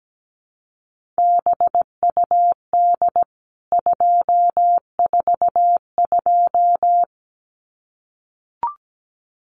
Morse code, heard as a keyed tone, about 17 wpm, 700 Hz. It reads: BUD 242
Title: CWT Exchange 17wpm